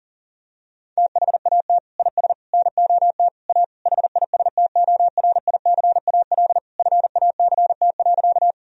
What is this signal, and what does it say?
Morse code, heard as a keyed tone, about 40 wpm, 700 Hz. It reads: THAT IS NOT A HISTORICAL FACT.
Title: Sentences from Top 1000 Words - Review Part 4 - 40wpm